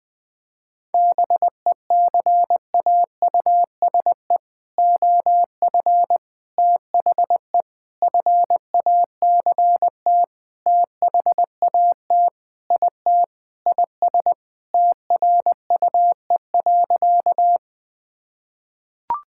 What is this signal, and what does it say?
Morse code, heard as a keyed tone, about 21 wpm, 700 Hz. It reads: BECAUSE OF THE FACT THAT IT IS TRUE.
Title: Sentences with 4-Word Phrases - Spoken Beginning - 20wpm